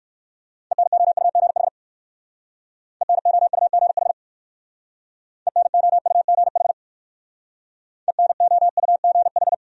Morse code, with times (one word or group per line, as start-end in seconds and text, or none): 0.7-1.7 ENOUGH
3.0-4.1 ENOUGH
5.5-6.7 ENOUGH
8.1-9.5 ENOUGH